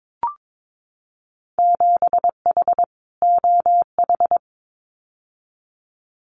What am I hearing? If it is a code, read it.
Morse code, heard as a keyed tone, about 22 wpm, 700 Hz. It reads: 7H OH
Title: Winter Field Day Exchanges Encore 22wpm